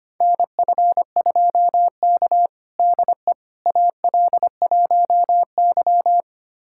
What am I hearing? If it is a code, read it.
Morse code, heard as a keyed tone, about 25 wpm, 700 Hz. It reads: NF2K DE AL1Y
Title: USA Callsign DE Callsign - 4 Characters - 25wpm